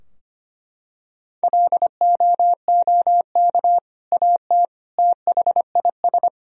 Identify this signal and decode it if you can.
Morse code, heard as a keyed tone, about 25 wpm, 700 Hz. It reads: LOOK AT THIS
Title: Sentences from Top 1000 Words - Review Part 2 - 25wpm